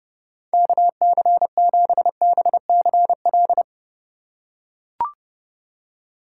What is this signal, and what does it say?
Morse code, heard as a keyed tone, about 32 wpm, 700 Hz. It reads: KC7BCL
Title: USA Call Signs - 6 Characters - 3x Character Spacing - 30wpm